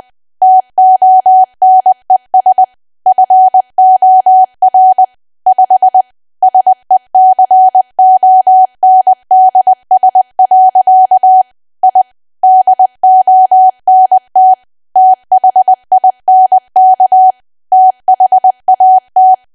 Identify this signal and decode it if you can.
Morse code, heard as a keyed tone, about 20 wpm, 750 Hz. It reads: TONES FOR 5 SECONDS. I DONT THINK THAT
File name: Archive / 20 WPM / 20WPM